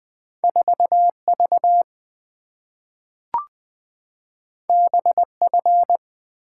Morse code, heard as a keyed tone, about 20 wpm, 700 Hz. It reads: 4V BF
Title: Two Letter-Number Combinations - 20wpm